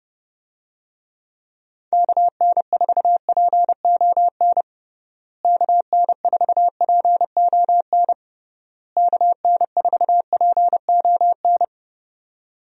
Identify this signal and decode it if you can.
Morse code, heard as a keyed tone, about 32 wpm, 700 Hz. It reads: KN4PON KN4PON KN4PON
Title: USA Call Signs - 6 Characters - Repeated 3x - 30wpm